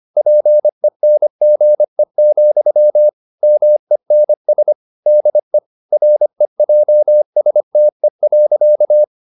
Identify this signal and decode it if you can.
Morse code, heard as a keyed tone, about 25 wpm, 600 Hz. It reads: PENGE, MENS DE REJSTE.
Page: Ferie 25 wpm